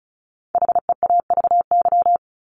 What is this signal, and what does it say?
Morse code, heard as a keyed tone, about 34 wpm, 700 Hz. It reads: HEAVY
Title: LY Suffix - Top 100 Words with Base Words 35wpm